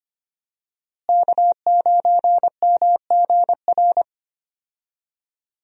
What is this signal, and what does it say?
Morse code, heard as a keyed tone, about 25 wpm, 700 Hz. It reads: K9MGR
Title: USA Call Signs - 5 Characters - Repeated 3x - 25wpm